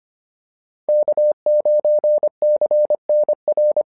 Morse code, heard as a keyed tone, about 25 wpm, 590 Hz. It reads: K9CNR
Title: SOTA QSO Practice 25wpm